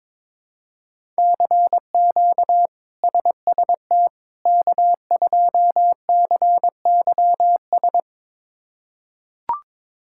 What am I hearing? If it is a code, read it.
Morse code, heard as a keyed tone, about 22 wpm, 700 Hz. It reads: CQ SST K2CYS
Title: K1USN SST QSO Practice - 22wpm